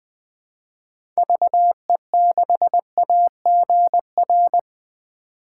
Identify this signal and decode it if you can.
Morse code, heard as a keyed tone, about 20 wpm, 700 Hz. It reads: VE6AGR
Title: 2023 SOTA Chaser Call Signs - 20wpm